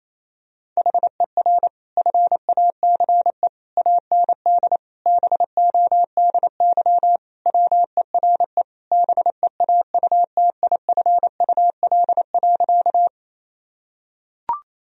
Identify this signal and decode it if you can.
Morse code, heard as a keyed tone, about 27 wpm, 690 Hz. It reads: HER FACE AND BODY WERE BEAUTIFUL.
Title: Sentences from Top 1000 Words - Review Part 2 - 28wpm